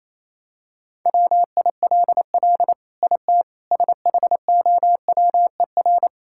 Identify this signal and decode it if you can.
Morse code, heard as a keyed tone, about 28 wpm, 700 Hz. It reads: WILL IT SHOWER
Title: Sentences from Top 1600 Words 28wpm